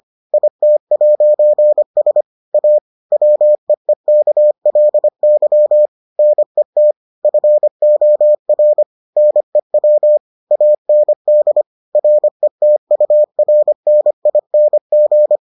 Morse code, heard as a keyed tone, about 25 wpm, 600 Hz. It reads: IT'S A WEEKLY NET FOR NEW AND RETURNING